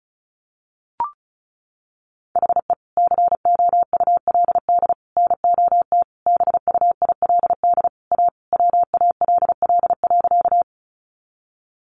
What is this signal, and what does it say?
Morse code, heard as a keyed tone, about 34 wpm, 700 Hz. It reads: HE COULD NOT BUILD A WALL.
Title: Sentences from Top 1100 Words 35wpm